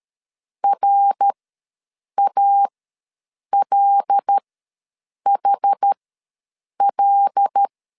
Morse code, heard as a keyed tone, about 13 wpm, spaced as for 7 wpm, 800 Hz. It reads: RALHL
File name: LEZIONI / 5 LBVC / 5 L ws 12wpm 2958